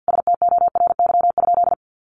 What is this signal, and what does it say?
Morse code, heard as a keyed tone, about 50 wpm, 700 Hz. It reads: STORY?
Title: Sentences from Top 400 Words 50wpm